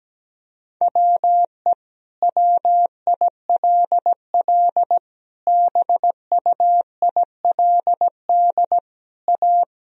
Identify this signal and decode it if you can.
Morse code, heard as a keyed tone, about 17 wpm, 700 Hz. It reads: WE WILL BUILD A
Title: Sentences from Top 1000 Words - Review Part 4 - 17wpm